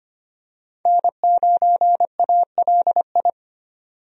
Morse code, heard as a keyed tone, about 25 wpm, 690 Hz. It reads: N9ALI
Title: USA Call Signs - 5 Characters - 25wpm